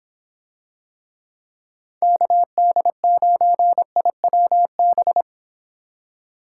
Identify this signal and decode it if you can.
Morse code, heard as a keyed tone, about 24 wpm, 700 Hz. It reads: KD9IWB